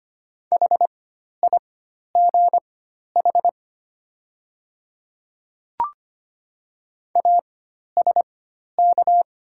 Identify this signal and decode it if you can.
Morse code, heard as a keyed tone, about 25 wpm, spaced as for 12 wpm, 690 Hz. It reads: HIGH ASK
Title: 4x Character Spacing - Top 200 Words - 25wpm